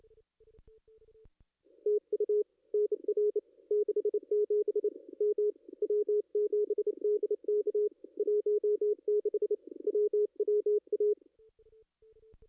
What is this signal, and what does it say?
Morse code, heard as a keyed tone, about 28 wpm, 420 Hz. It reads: TU NR 67 M W7DK 16 WWA
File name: contests / ss-cw-k5tr-op-n5rz -2200